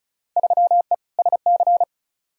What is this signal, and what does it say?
Morse code, heard as a keyed tone, about 34 wpm, 700 Hz. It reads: 3E SC
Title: Field Day Exchanges Encore - 35wpm